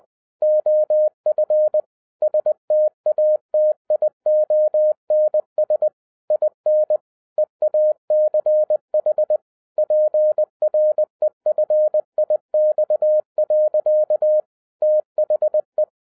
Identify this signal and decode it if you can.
Morse code, heard as a keyed tone, about 20 wpm, 610 Hz. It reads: OF STATIONS IN EACH PREFIX. THE